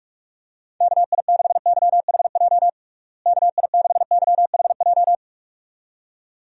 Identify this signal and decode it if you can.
Morse code, heard as a keyed tone, about 44 wpm, 700 Hz. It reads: KI6YHJ KI6YHJ
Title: USA Call Signs - 6 Characters - Repeated 2x - 45wpm